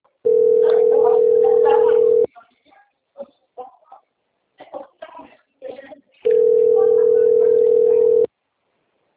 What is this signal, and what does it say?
Ringback tone (North America), 440 + 480 Hz, with speech.
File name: out- - - - - .60288